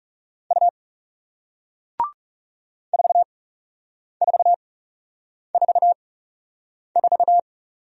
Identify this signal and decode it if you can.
Morse code, heard as a keyed tone, about 41 wpm, spaced as for 8 wpm, 700 Hz. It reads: U 4444